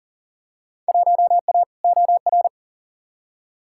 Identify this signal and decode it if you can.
Morse code, heard as a keyed tone, about 40 wpm, 700 Hz. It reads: 1A OR
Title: Field Day Exchanges - 40wpm